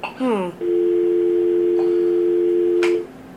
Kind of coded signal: telephone call progress tone